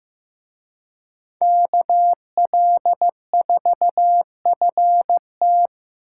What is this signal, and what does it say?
Morse code, heard as a keyed tone, about 18 wpm, 700 Hz. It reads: KL4FT
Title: Speed Racing - USA Call Signs 15wpm